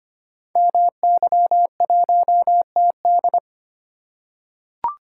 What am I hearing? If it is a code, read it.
Morse code, heard as a keyed tone, about 25 wpm, 700 Hz. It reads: MY1TD
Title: Code Groups - 5 Letter:Numbers - 25wpm